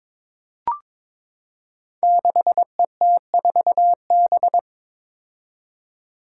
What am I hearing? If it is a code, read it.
Morse code, heard as a keyed tone, about 21 wpm, 700 Hz. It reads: 6ET4B
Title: Five Challenging Letter-Number Combinations 22wpm